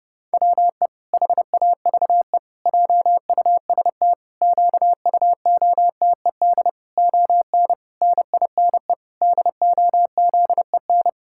Morse code, heard as a keyed tone, about 30 wpm, 700 Hz. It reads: WE HAVE JUST QUOTED ON NINE DOZEN
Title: Pangram Challenge v2 - 30wpm